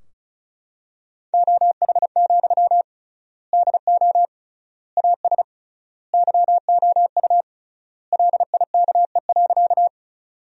Morse code, heard as a keyed tone, about 36 wpm, 700 Hz. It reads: OH, DO AS YOU LIKE.
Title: 3x Word-Spacing - Sentences from Top 500 Words - 35wpm